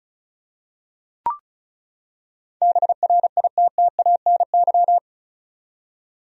Morse code, heard as a keyed tone, about 34 wpm, 700 Hz. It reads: BRITTANY
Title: USA Common Names - 35wpm